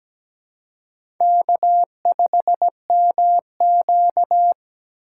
Morse code, heard as a keyed tone, about 17 wpm, 700 Hz. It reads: K5MQ
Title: USA Call Signs - 3x Character Spacing - 17wpm